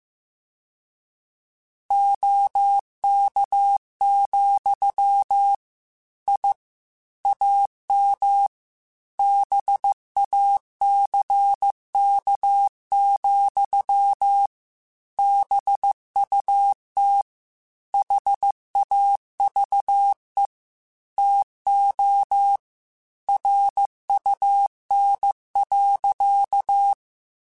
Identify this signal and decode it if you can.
Morse code, heard as a keyed tone, about 15 wpm, 800 Hz. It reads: OK, I AM BACK, BUT HAVE TO RUN.
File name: QSOs / QSOs-mp3s / kc7sac@13